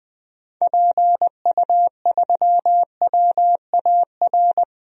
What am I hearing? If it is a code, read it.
Morse code, heard as a keyed tone, about 20 wpm, 700 Hz. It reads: PU3WAR
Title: DXCC-100 Call Signs - 6 Characters - 20wpm